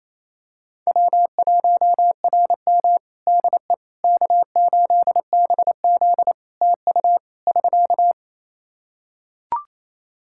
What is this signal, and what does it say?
Morse code, heard as a keyed tone, about 28 wpm, 700 Hz. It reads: W1RM DE K8BZ TU <SK>
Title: Practice QSO - Exchange 4 - Ending - 28wpm